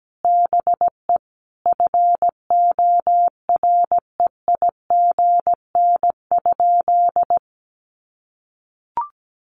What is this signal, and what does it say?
Morse code, heard as a keyed tone, about 17 wpm, 700 Hz. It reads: BE FOREIGN?
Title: Sentences from Top 1400 Words 17wpm